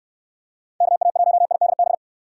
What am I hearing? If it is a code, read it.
Morse code, heard as a keyed tone, about 66 wpm, 700 Hz. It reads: KA1IDB